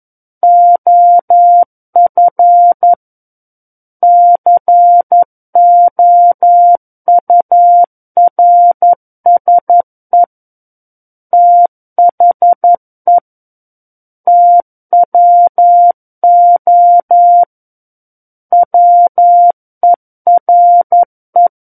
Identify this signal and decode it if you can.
Morse code, heard as a keyed tone, about 11 wpm, 700 Hz. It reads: OF COURSE THE TWO WERE
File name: 11 WPM